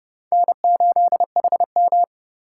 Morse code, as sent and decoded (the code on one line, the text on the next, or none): -. ---.. .... --
N8HM